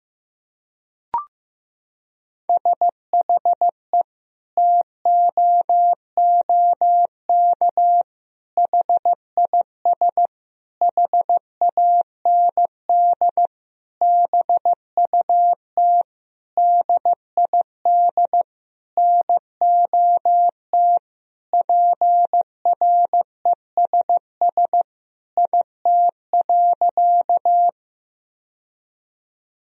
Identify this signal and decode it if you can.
Morse code, heard as a keyed tone, about 16 wpm, 700 Hz. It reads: SHE TOOK HIS HAND BUT DID NOT PRESS IT.
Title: Sentences from Top 1000 Words - Review Part 1 - 15wpm